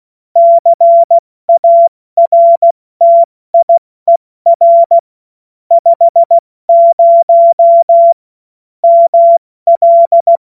Morse code, heard as a keyed tone, about 16 wpm, spaced as for 14 wpm, 680 Hz. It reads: CARTIER 50 ML
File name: Morsekurs / Witze / Witz 31. August 2021, 70 F80